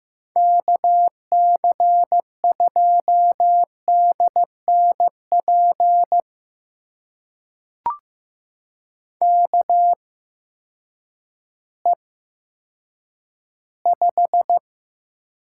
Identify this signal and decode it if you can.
Morse code, heard as a keyed tone, about 15 wpm, spaced as for 9 wpm, 700 Hz. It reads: KC2DNP KE5
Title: USA Call Signs - 6 Characters - 8x Character Spacing - 15wpm